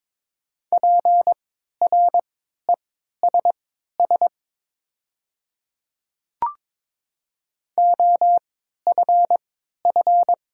Morse code, heard as a keyed tone, about 22 wpm, spaced as for 12 wpm, 700 Hz. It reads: PRESS OFF
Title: 3x Character Spacing - Top 300 Words - 22wpm